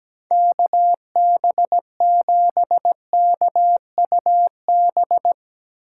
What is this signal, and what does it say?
Morse code, heard as a keyed tone, about 17 wpm, 700 Hz. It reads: KB7KUB
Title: USA Call Signs - 6 Characters - 6x Character Spacing - 17wpm